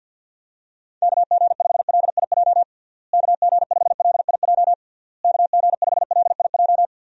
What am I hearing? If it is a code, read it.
Morse code, heard as a keyed tone, about 50 wpm, 700 Hz. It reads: KG5LIJ KG5LIJ KG5LIJ
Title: USA Call Signs - 6 Characters - Repeated 3x - 50wpm